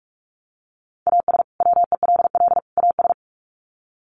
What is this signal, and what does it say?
Morse code, heard as a keyed tone, about 44 wpm, 700 Hz. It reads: AS WELL AS
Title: Sets of 3 Words 45wpm